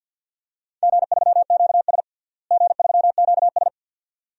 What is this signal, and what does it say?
Morse code, heard as a keyed tone, about 52 wpm, 700 Hz. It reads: G3QS G3QS
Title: DXCC-100 Call Signs - 4 Characters - Repeated 2x - 50wpm